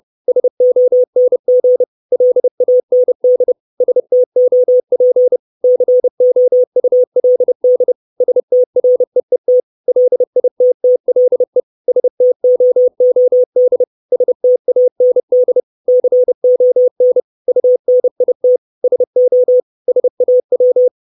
Morse code, heard as a keyed tone, about 30 wpm, 500 Hz. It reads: SONG LAND STOP COULD STREET LITTLE STOOD STAND CON UNIT SO SAW
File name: 632 Words 30WPM